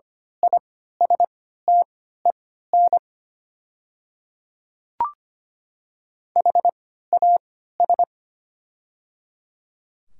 Morse code, heard as a keyed tone, about 25 wpm, spaced as for 14 wpm, 700 Hz. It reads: ISTEN HAS
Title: 3x Character Spacing - Top 500 Words - 25wpm